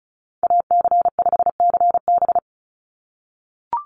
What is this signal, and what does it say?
Morse code, heard as a keyed tone, about 34 wpm, 700 Hz. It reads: AC5CB
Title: USA Call Signs 35wpm